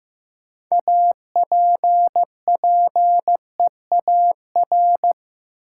Morse code, heard as a keyed tone, about 15 wpm, 700 Hz. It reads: APPEAR
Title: Sentences from Top 1000 Words - Review Part 2 - New Word Drill - 15wpm